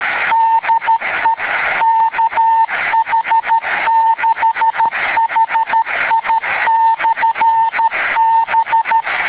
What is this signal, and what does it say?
Morse code, heard as a keyed tone, about 13 wpm, 900 Hz. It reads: DE KH6HI/B